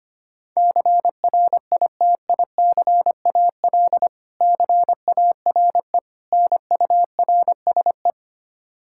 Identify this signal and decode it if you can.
Morse code, heard as a keyed tone, about 25 wpm, 700 Hz. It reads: CRITICAL CARE NURSE
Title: Practice QSO - Exchange 3 - More Info - 25wpm